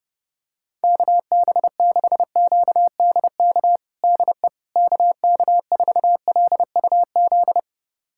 Morse code, heard as a keyed tone, about 32 wpm, 700 Hz. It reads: KB6QDK DE KK4LUZ
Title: USA Callsign DE Callsign 30wpm